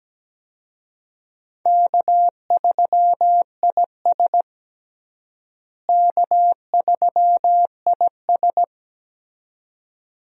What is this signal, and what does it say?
Morse code, heard as a keyed tone, about 17 wpm, 700 Hz. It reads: K3IS K3IS
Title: USA Call Signs - 4 Characters - Repeated 2x - 17wpm